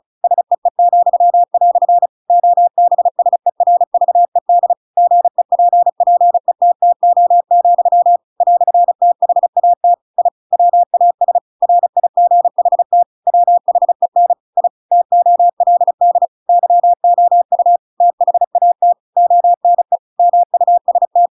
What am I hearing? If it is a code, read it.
Morse code, heard as a keyed tone, about 34 wpm, 700 Hz. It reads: SEE," OBSERVED GEPPETTO, "THAT I WAS RIGHT WHEN I TOLD YOU THAT ONE MUST
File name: Pinocchio / Pinocchio ch7 35 35 ik1hge